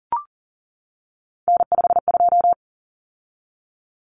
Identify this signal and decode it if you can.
Morse code, heard as a keyed tone, about 40 wpm, 700 Hz. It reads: N52
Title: Common Cut Numbers - 3-Digit 40wpm